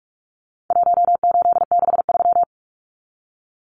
Morse code, heard as a keyed tone, about 44 wpm, 700 Hz. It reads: 1863
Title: Numbers - Contest Style - 4-Digit 45wpm